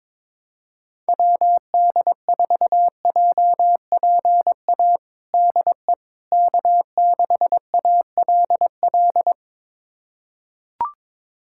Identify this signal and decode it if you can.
Morse code, heard as a keyed tone, about 22 wpm, 700 Hz. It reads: WD4JPA DE K6ALL
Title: USA Callsign DE Callsign 22wpm